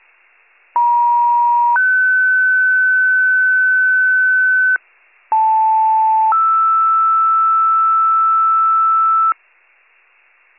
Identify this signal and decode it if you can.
Two-tone page, heard as a one-second tone, then a three-second tone, about 950 then 1500 Hz, about 880 then 1320 Hz. It is heard from 0.8 to 9.3 s.